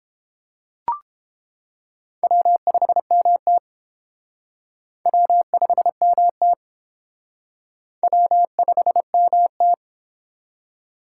Morse code, heard as a keyed tone, about 29 wpm, 700 Hz. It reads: W5MT W5MT W5MT